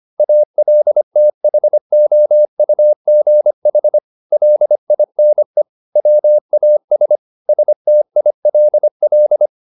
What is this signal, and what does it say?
Morse code, heard as a keyed tone, about 25 wpm, 600 Hz. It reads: ALTHOUGH LINE WAS STILL